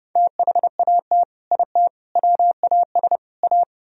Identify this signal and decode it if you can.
Morse code, heard as a keyed tone, about 28 wpm, 700 Hz. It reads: THAT IT WAS A